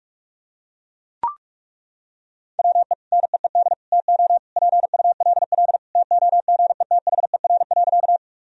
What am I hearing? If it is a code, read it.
Morse code, heard as a keyed tone, about 42 wpm, 700 Hz. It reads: WE NEED TO PULL TOGETHER.
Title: Sentences from Top 1000 Words - Review Part 2 - 45wpm